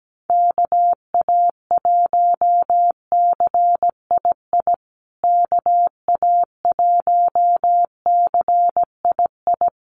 Morse code, heard as a keyed tone, about 17 wpm, 700 Hz. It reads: KA1CII KA1CII
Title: USA Call Signs Repeated 3x 17wpm